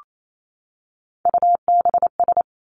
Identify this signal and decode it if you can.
Morse code, heard as a keyed tone, about 28 wpm, 700 Hz. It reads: UBS
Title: Three Challenging Letter-Number Combinations 28wpm